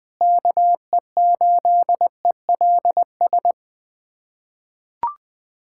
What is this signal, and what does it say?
Morse code, heard as a keyed tone, about 21 wpm, 700 Hz. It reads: KE8ELS
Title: USA Callsign DE Callsign - 6 Characters - 20wpm